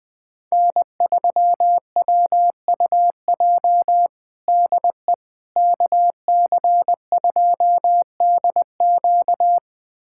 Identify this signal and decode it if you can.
Morse code, heard as a keyed tone, about 20 wpm, 700 Hz. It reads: N3WUJ DE KC2DQ
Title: USA Callsign DE Callsign - 5 Characters - 20wpm